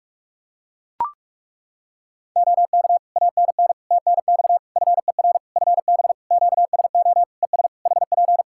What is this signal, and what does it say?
Morse code, heard as a keyed tone, about 41 wpm, 700 Hz. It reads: OK ANN TNX FER FB QSO ES HP
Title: K1USN SST QSO Practice 45wpm